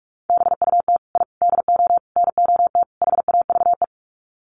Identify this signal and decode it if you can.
Morse code, heard as a keyed tone, about 44 wpm, 700 Hz. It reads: BUT I DO NOT HAVE
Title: Sets of 5 Words Encore 45wpm